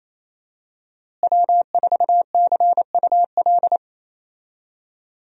Morse code, heard as a keyed tone, about 28 wpm, 700 Hz. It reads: W4CUL
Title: USA Call Signs - 5 Characters - Repeated 2x - 28wpm